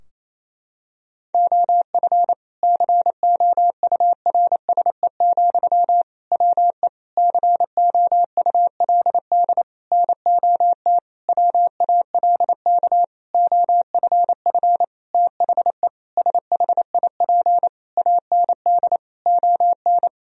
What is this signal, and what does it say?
Morse code, heard as a keyed tone, about 28 wpm, 700 Hz. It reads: OF COURSE, WE COULD NOT WALK OFF THE SHIP AND ON
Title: Sentences from Top 1000 Words - Review Part 2 - 28wpm